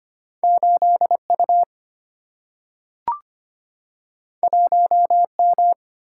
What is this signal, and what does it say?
Morse code, heard as a keyed tone, about 25 wpm, 700 Hz. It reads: 8 U 1 M
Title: Single-Letter Single-Number 25wpm